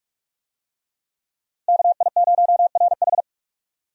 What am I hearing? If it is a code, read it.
Morse code, heard as a keyed tone, about 46 wpm, 700 Hz. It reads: KI0RH
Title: USA Call Signs - 5 Characters - Repeated 2x - 45wpm